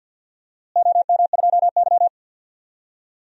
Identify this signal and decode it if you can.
Morse code, heard as a keyed tone, about 52 wpm, 700 Hz. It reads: OM2Y
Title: DXCC-100 Call Signs - 4 Characters - Repeated 2x - 50wpm